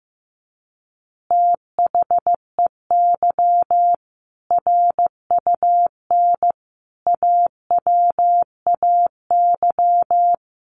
Morse code, heard as a keyed tone, about 15 wpm, 700 Hz. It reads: THEY RUN AWAY
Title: Easy Phrases 15wpm